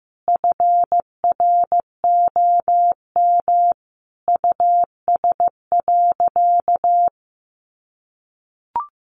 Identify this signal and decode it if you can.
Morse code, heard as a keyed tone, about 15 wpm, 700 Hz. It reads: FROM US.
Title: Sentences from Top 700 Words 15wpm